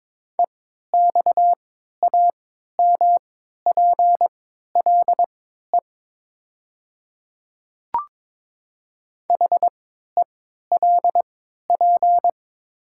Morse code, heard as a keyed tone, about 22 wpm, spaced as for 12 wpm, 700 Hz. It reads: EXAMPLE HELP
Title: 3x Character Spacing - Top 300 Words - 22wpm